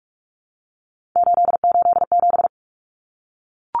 Morse code, heard as a keyed tone, about 44 wpm, 700 Hz. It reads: 887
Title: Numbers - 3-Digit 45wpm